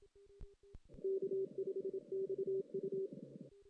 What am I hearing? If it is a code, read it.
Morse code, heard as a keyed tone, about 27 wpm, 390 Hz. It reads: K5XU
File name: 2016 JAN CW NAQP